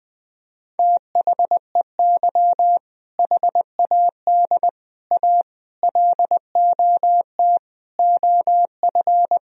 Morse code, heard as a keyed tone, about 21 wpm, 700 Hz. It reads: THEY HAD A LOT OF
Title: Sentences from Top 1000 Words - Review Part 3 - 20wpm